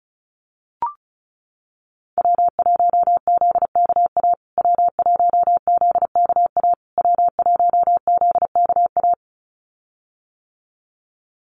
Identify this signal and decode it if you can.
Morse code, heard as a keyed tone, about 34 wpm, 700 Hz. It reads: W1ZKA W1ZKA W1ZKA
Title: USA Call Signs Repeated 3x 35wpm